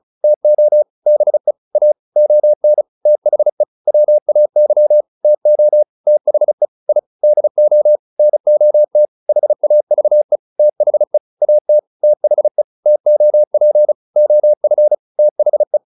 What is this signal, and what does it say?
Morse code, heard as a keyed tone, about 35 wpm, 600 Hz. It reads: TO BE A ON THE WAY TO THE I DO NOT HAVE THE AT THE TOP OF THE